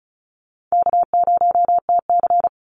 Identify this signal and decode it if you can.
Morse code, heard as a keyed tone, about 34 wpm, 700 Hz. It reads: K0TC
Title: Easy Call Signs Repeated 2x - 35wpm